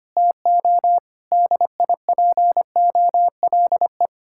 Morse code, heard as a keyed tone, about 31 wpm, 700 Hz. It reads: TO DIPOLE